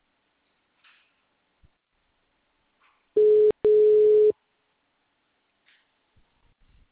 Ringback tone (Europe, 425 Hz).